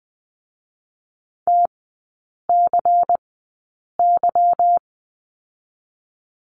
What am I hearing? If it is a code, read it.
Morse code, heard as a keyed tone, about 20 wpm, 700 Hz. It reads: T C Y
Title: 205 Lesson 33e - Letter j - Three Characters 20wpm